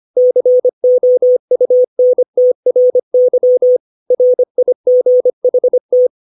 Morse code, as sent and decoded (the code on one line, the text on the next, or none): -.-. --- ..- -. - .-. -.-- / .-. .. --. .... -
COUNTRY RIGHT